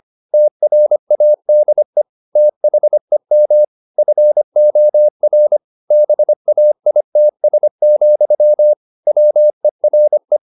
Morse code, heard as a keyed tone, about 25 wpm, 600 Hz. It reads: TRADE THEM FOR BAITS, WERE
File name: the old man and the sea part 1 25wpm